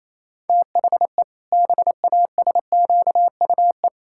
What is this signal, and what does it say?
Morse code, heard as a keyed tone, about 28 wpm, 700 Hz. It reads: THE BASQUE
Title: Pangram Challenge v2 - 28wpm